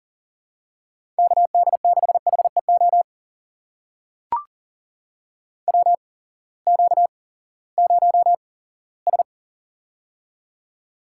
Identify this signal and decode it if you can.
Morse code, heard as a keyed tone, about 40 wpm, spaced as for 20 wpm, 700 Hz. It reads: KD6HEO WQ0S
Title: USA Call Signs - 8x Character Spacing - 40wpm